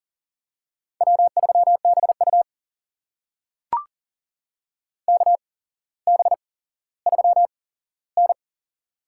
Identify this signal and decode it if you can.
Morse code, heard as a keyed tone, about 40 wpm, spaced as for 17 wpm, 700 Hz. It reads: W3BU KB3N
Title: USA Call Signs - 4 Characters - 8x Character Spacing - 40wpm